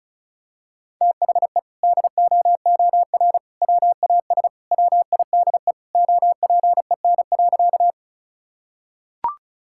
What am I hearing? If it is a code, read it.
Morse code, heard as a keyed tone, about 36 wpm, 700 Hz. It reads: THE DOOR WAS WIDE OPEN.
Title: Sentences from Top 1000 Words - Review Part 6 - 35wpm